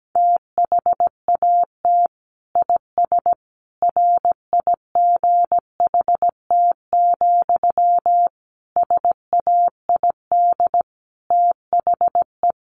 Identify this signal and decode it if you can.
Morse code, heard as a keyed tone, about 17 wpm, 700 Hz. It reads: THAT IS RIGHT, SAID THE
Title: Sentences from Top 200 Words 17wpm